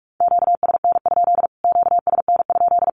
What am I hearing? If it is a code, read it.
Morse code, heard as a keyed tone, about 44 wpm, 700 Hz. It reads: QSN? QSN?
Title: Q-Codes Repeated 3x 45wpm